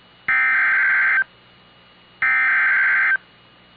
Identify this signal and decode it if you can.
Emergency Alert System (SAME) tones (header bursts), heard as harsh buzzing bursts. It is heard from 0.3 to 3.1 s.